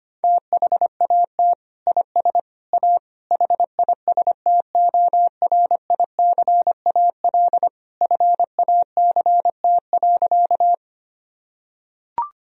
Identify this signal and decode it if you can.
Morse code, heard as a keyed tone, about 25 wpm, 700 Hz. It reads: THAT IS A HISTORICAL FACT.
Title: Sentences from Top 1000 Words - Review Part 4 - 25wpm